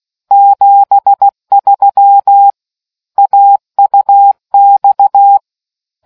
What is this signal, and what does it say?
Morse code, heard as a keyed tone, about 16 wpm, 800 Hz. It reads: 73 AUX